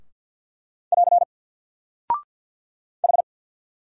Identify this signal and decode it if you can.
Morse code, heard as a keyed tone, about 50 wpm, 700 Hz. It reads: <AR> H
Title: Single Character (Letter:Number:Punctuation:Prosign) 50wpm